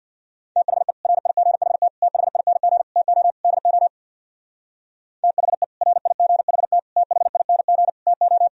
Morse code, heard as a keyed tone, about 64 wpm, 700 Hz. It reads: THE RIGHT THING TO DO THE RIGHT THING TO